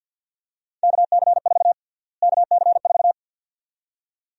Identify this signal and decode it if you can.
Morse code, heard as a keyed tone, about 50 wpm, 700 Hz. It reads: KC4 KC4
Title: USA Call Sign Prefixes - Repeated 2x - 50wpm